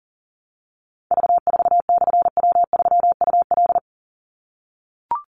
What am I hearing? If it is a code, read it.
Morse code, heard as a keyed tone, about 40 wpm, 700 Hz. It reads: V4/W3UL
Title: Call Signs - 40wpm